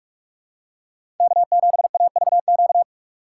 Morse code, heard as a keyed tone, about 46 wpm, 700 Hz. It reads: K7AVQ
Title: USA Call Signs - 5 Characters - 6x Character Spacing - 45wpm